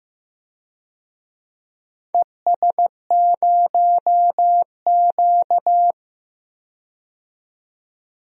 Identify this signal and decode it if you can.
Morse code, heard as a keyed tone, about 15 wpm, 700 Hz. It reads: ES0Q
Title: DXCC-100 Call Signs - 4 Characters - 15wpm